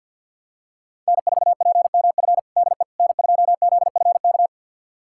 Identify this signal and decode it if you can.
Morse code, heard as a keyed tone, about 51 wpm, 700 Hz. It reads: N4PMF DE N2ZFK
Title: USA Callsign DE Callsign - 5 Characters - 50wpm